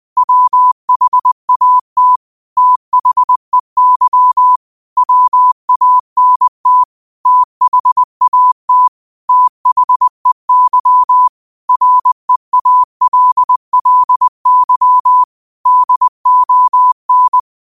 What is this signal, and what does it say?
Morse code, heard as a keyed tone, about 20 wpm, 1000 Hz. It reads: WHAT THEY WANT THAT THEY REALLY DON